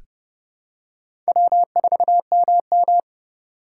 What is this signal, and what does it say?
Morse code, heard as a keyed tone, about 30 wpm, 700 Hz. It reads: W4MM
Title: USA Call Signs - 4 Characters - 3x Character Spacing - 30wpm